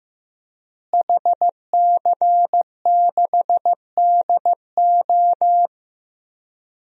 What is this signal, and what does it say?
Morse code, heard as a keyed tone, about 15 wpm, 700 Hz. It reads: HC6DO
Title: DXCC-100 Call Signs - 5 Characters - Repeated 3x - 15wpm